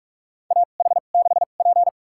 Morse code, heard as a keyed tone, about 46 wpm, spaced as for 33 wpm, 700 Hz. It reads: AH6P
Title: USA Call Signs - 2x Character Spacing - 45wpm